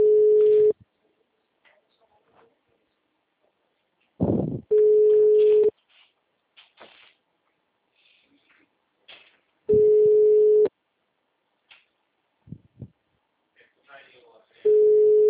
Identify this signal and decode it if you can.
Ringback tone (Europe, 425 Hz), with speech.